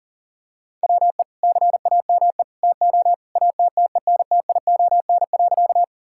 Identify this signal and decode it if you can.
Morse code, heard as a keyed tone, about 37 wpm, 700 Hz. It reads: WE CAME TO ATTENTION.
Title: Sentences from Top 1000 Words - Review Part 6 - 40wpm